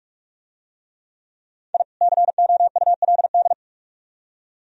Morse code, heard as a keyed tone, about 43 wpm, 700 Hz. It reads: I COULD